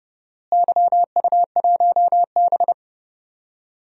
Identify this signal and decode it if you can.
Morse code, heard as a keyed tone, about 30 wpm, 700 Hz. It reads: YU1B
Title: DXCC-100 Call Signs - 4 Characters - Repeated 3x - 30wpm